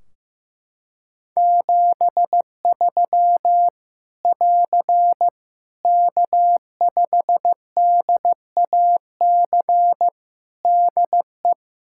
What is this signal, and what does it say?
Morse code, heard as a keyed tone, about 15 wpm, 700 Hz. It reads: 73 <AR> K5DAC DE
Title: Practice QSO - Exchange 4 - Ending - 15wpm